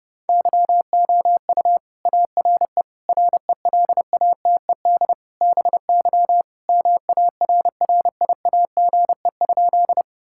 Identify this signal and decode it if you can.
Morse code, heard as a keyed tone, about 30 wpm, 700 Hz. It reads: YOU ARE RELATED BY MARRIAGE?
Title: Sentences from Top 1000 Words - Review Part 5 - 30wpm